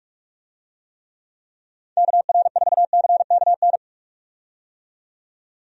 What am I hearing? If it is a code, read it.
Morse code, heard as a keyed tone, about 46 wpm, 700 Hz. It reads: KR4CKN
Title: USA Call Signs - 6 Characters - 45wpm